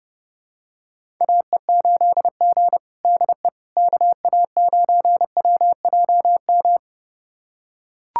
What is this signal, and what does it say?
Morse code, heard as a keyed tone, about 32 wpm, 700 Hz. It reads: AE8G DE KA9WJM
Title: USA Callsign DE Callsign 30wpm